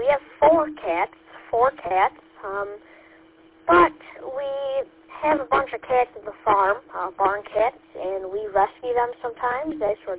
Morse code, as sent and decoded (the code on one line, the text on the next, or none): none